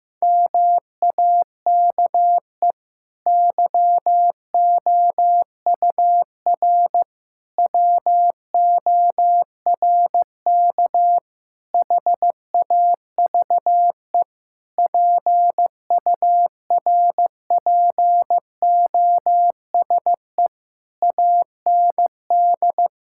Morse code, heard as a keyed tone, about 15 wpm, 700 Hz. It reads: MAKE YOUR WORK HAVE PURPOSE AND
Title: Sentences from Top 1000 Words - Review Part 6 - 15wpm